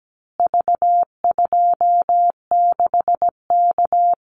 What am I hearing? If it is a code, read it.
Morse code, heard as a keyed tone, about 17 wpm, 700 Hz. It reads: V26K
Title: Call Signs Repeated 2x - 17wpm